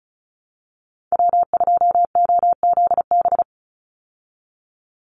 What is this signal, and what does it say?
Morse code, heard as a keyed tone, about 34 wpm, 700 Hz. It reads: W2OZB
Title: Call Signs Repeated 2x - 35wpm